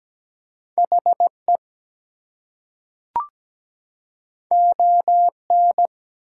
Morse code, heard as a keyed tone, about 17 wpm, 700 Hz. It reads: HE ON
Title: Two Letter Words - 17wpm